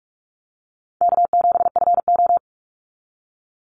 Morse code, heard as a keyed tone, about 46 wpm, 700 Hz. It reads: K7FO
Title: NA SOTA Chaser Callsigns 45wpm